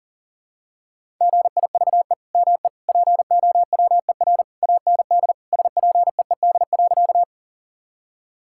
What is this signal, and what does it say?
Morse code, heard as a keyed tone, about 40 wpm, 700 Hz. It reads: GIVE ME POWER AND SPEED.
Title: Sentences from Top 1000 Words - Review Part 5 - 40wpm